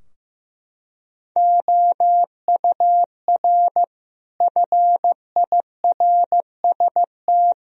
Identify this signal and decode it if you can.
Morse code, heard as a keyed tone, about 15 wpm, 700 Hz. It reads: OUR FIRST
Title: Sentences from Top 1000 Words - Review Part 3 - 15wpm